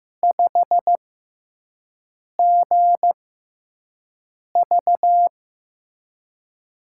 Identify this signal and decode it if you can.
Morse code, heard as a keyed tone, about 15 wpm, spaced as for 5 wpm, 700 Hz. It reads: 5GV
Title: USA Call Signs - 6x Character Spacing - 15wpm